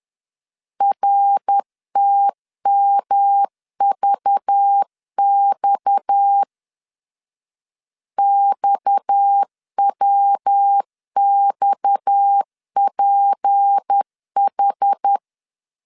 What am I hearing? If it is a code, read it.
Morse code, heard as a keyed tone, about 10 wpm, 800 Hz. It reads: RTMVX XWXPH
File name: LEZIONI / 6 PKX0 / 6 PKX 10wpm 2426